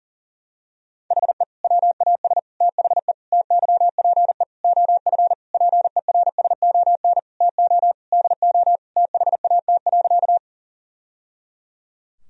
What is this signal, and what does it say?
Morse code, heard as a keyed tone, about 40 wpm, 700 Hz. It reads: HE WAS THE TYPE OF PERSON TO DO THAT.
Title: Sentences from Top 1000 Words - Review Part 3 - 40wpm